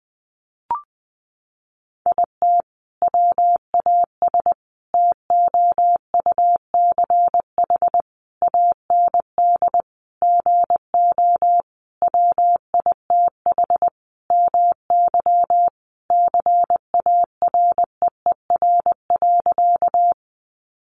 Morse code, heard as a keyed tone, about 20 wpm, 700 Hz. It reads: IT WAS TOUCH AND GO WITH MY CAREER.
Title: Sentences from Top 1200 Words 20wpm